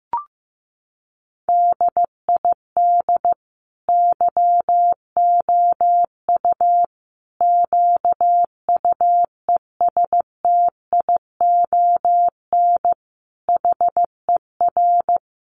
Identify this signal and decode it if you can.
Morse code, heard as a keyed tone, about 15 wpm, 700 Hz. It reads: DID YOU QUESTION HER
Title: Sentences from Top 800 Words 15wpm